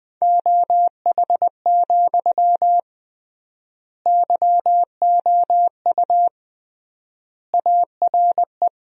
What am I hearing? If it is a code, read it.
Morse code, heard as a keyed tone, about 20 wpm, 700 Hz. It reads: OH, YOU ARE
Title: 3x Word-Spacing - Sentences from Top 500 Words - 20wpm